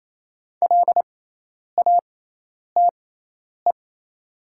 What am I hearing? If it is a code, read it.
Morse code, heard as a keyed tone, about 27 wpm, spaced as for 10 wpm, 690 Hz. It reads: LATE
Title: 6x Character Spacing - Top 500 Words - 28wpm